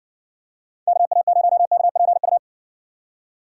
Morse code, heard as a keyed tone, about 57 wpm, 700 Hz. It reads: KA0KPU